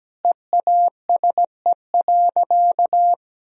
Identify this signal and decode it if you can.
Morse code, heard as a keyed tone, about 17 wpm, 700 Hz. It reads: EASE.
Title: Sentences from Top 1000 Words - Review Part 1 - 17wpm